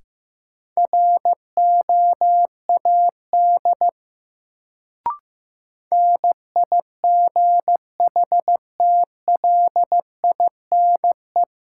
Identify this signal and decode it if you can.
Morse code, heard as a keyed tone, about 15 wpm, 700 Hz. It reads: ROAD NIGHTLINE
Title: Sets of 5 Words - New Word Drill - 15wpm